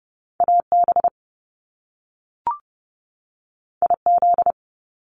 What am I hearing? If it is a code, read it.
Morse code, heard as a keyed tone, about 30 wpm, 700 Hz. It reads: AB IZ
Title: Top Two Letter Combinations in Words 30wpm